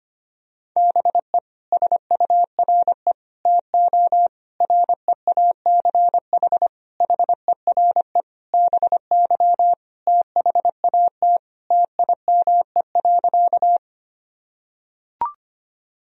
Morse code, heard as a keyed tone, about 25 wpm, 700 Hz. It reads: BE SURE TO REACH HERE BY THAT TIME.
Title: Sentences from Top 1000 Words - Review Part 2 - 25wpm